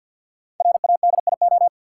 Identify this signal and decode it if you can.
Morse code, heard as a keyed tone, about 53 wpm, 700 Hz. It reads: RADIO
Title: Sentences from Top 1000 Words - Review Part 3 - New Word Drill - 50wpm